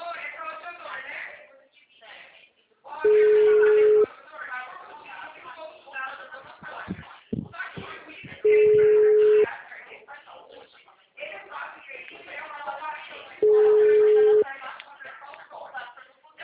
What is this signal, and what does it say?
Ringback tone (Europe, 425 Hz), with speech.